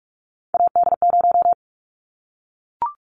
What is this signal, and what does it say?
Morse code, heard as a keyed tone, about 44 wpm, 700 Hz. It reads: AD0
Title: Callsign Prefixes for Top 25 DXCC - Repeated 3x - 45wpm